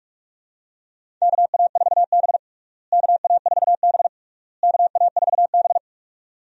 Morse code, heard as a keyed tone, about 41 wpm, 700 Hz. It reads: KA4B KA4B KA4B
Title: USA Call Signs - 4 Characters - Repeated 3x - 45wpm